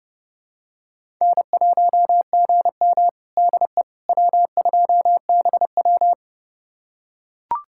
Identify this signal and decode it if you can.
Morse code, heard as a keyed tone, about 32 wpm, 700 Hz. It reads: N1GM DE W2BW
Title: USA Callsign DE Callsign - 4 Characters - 30wpm